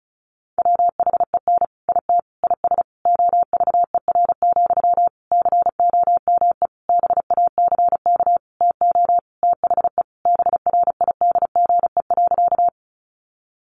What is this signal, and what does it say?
Morse code, heard as a keyed tone, about 34 wpm, 700 Hz. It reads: WHEN IT IS OVER, COME BACK TO THE BRIDGE.
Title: Sentences from Top 1200 Words 35wpm